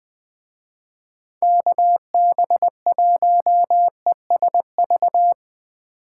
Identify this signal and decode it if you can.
Morse code, heard as a keyed tone, about 20 wpm, 700 Hz. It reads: KB1ESV
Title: Speed Racing - USA 2x3 Call Signs 17wpm